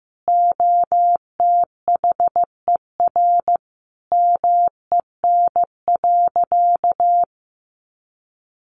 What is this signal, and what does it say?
Morse code, heard as a keyed tone, about 15 wpm, 700 Hz. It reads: OTHER MEN.
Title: Sentences from Top 1100 Words 15wpm